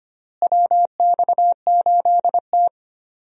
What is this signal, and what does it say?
Morse code, heard as a keyed tone, about 25 wpm, 700 Hz. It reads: WX8T
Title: USA Call Signs - 4 Characters - Repeated 3x - 25wpm